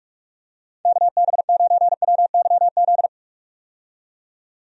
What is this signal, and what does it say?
Morse code, heard as a keyed tone, about 41 wpm, 700 Hz. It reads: KB9WYZ
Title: USA Call Signs - 6 Characters - 45wpm